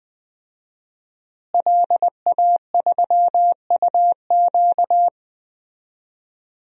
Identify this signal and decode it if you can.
Morse code, heard as a keyed tone, about 20 wpm, 700 Hz. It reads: LA3UQ
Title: DXCC-100 Call Signs - 5 Characters - Repeated 2x - 20wpm